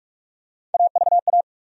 Morse code, heard as a keyed tone, about 42 wpm, 700 Hz. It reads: AVA
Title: USA Call Sign Postfixes - 3 Letters 45wpm